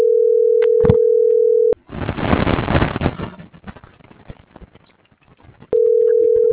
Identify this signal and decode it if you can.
Ringback tone (North America), 440 + 480 Hz.